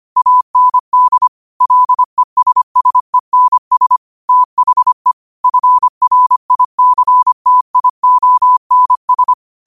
Morse code, heard as a keyed tone, about 25 wpm, 1000 Hz. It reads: AND LESSENS THE FRICTIONS